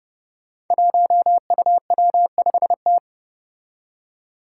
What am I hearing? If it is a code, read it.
Morse code, heard as a keyed tone, about 30 wpm, 700 Hz. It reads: 1UW5T
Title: Five Challenging Letter-Number Combinations 30wpm